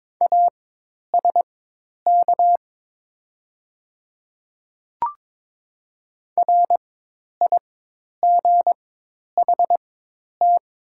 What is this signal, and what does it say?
Morse code, heard as a keyed tone, about 22 wpm, spaced as for 10 wpm, 700 Hz. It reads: ASK RIGHT
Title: 4x Character Spacing - Top 300 Words - 22wpm